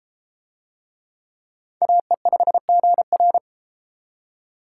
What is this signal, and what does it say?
Morse code, heard as a keyed tone, about 32 wpm, 700 Hz. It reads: AE5GR